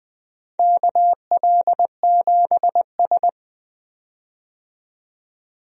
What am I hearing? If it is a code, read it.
Morse code, heard as a keyed tone, about 20 wpm, 700 Hz. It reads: KL7S
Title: DXCC-100 Call Signs - 4 Characters - 20wpm